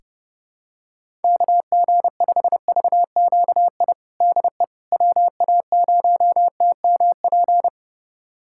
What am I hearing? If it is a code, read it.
Morse code, heard as a keyed tone, about 32 wpm, 700 Hz. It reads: KG5VQI DE WA0TMP
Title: USA Callsign DE Callsign 30wpm